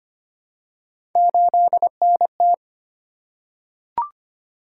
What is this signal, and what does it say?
Morse code, heard as a keyed tone, about 25 wpm, 700 Hz. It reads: 8NT
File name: Three Letter-Number Combinations - 25wpm